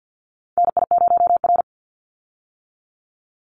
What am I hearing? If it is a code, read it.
Morse code, heard as a keyed tone, about 48 wpm, 700 Hz. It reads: NI0R
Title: Call Signs - 50wpm